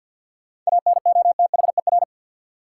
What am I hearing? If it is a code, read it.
Morse code, heard as a keyed tone, about 52 wpm, 700 Hz. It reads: ANOTHER
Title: Sentences from Top 600 Words - New Word Drill - 50wpm